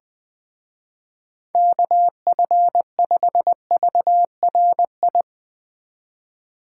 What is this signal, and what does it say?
Morse code, heard as a keyed tone, about 20 wpm, 700 Hz. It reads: KF5VRI
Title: Speed Racing - USA 2x3 Call Signs 17wpm